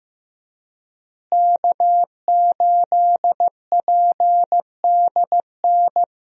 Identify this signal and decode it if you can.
Morse code, heard as a keyed tone, about 15 wpm, 700 Hz. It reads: K8PDN
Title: USA Call Signs - 5 Characters - Repeated 2x - 15wpm